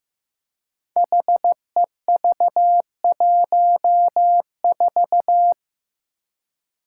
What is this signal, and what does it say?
Morse code, heard as a keyed tone, about 15 wpm, 700 Hz. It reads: HEV14
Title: Five Challenging Letter-Number Combinations 15wpm